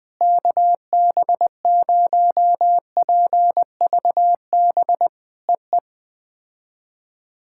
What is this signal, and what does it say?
Morse code, heard as a keyed tone, about 19 wpm, 700 Hz. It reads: KB0PVB EE
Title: POTA QSO Practice - 20wpm